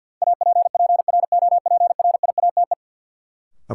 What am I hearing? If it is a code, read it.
Morse code, heard as a keyed tone, about 47 wpm, 700 Hz. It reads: APPROPRIATE
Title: Sentences from Top 900 Words - New Word Drill - 50wpm